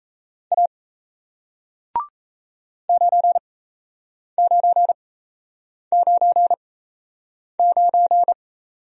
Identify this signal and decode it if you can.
Morse code, heard as a keyed tone, about 37 wpm, spaced as for 8 wpm, 700 Hz. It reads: A 9999